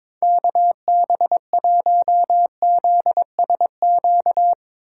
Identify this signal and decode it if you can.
Morse code, heard as a keyed tone, about 22 wpm, 700 Hz. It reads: KB1ZSQ
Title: Speed Racing - USA 2x3 Call Signs 22wpm